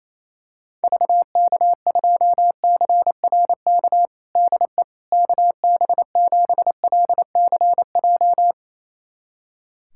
Morse code, heard as a keyed tone, about 28 wpm, 700 Hz. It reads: VK2CRK DE KB7LCJ
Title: USA Callsign DE Callsign 28wpm